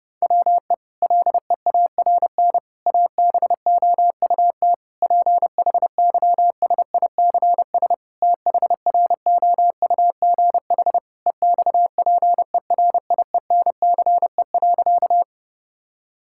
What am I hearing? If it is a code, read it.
Morse code, heard as a keyed tone, about 30 wpm, 700 Hz. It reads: WE LEARN ABOUT PHYSICS THROUGH EXPERIENCE.
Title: Sentences from Top 1000 Words - Review Part 5 - 30wpm